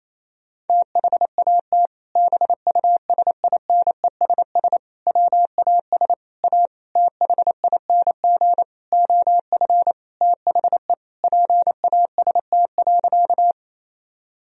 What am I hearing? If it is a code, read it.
Morse code, heard as a keyed tone, about 28 wpm, 700 Hz. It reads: THAT BUSINESS WAS A THING OF THE PAST.
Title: Sentences from Top 1000 Words - Review Part 3 - 28wpm